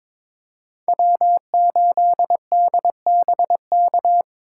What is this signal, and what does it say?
Morse code, heard as a keyed tone, about 22 wpm, 700 Hz. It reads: W8DBK
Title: USA Call Signs - 5 Characters - 3x Character Spacing - 22wpm